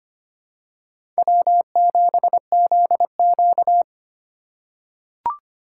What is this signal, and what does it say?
Morse code, heard as a keyed tone, about 25 wpm, 700 Hz. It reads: W7ZQ
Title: USA Call Signs - 4 Characters - 3x Character Spacing - 25wpm